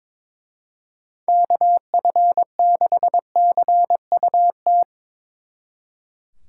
Morse code, heard as a keyed tone, about 24 wpm, 700 Hz. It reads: KF6CUT
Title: Speed Racing - USA 2x3 Call Signs 22wpm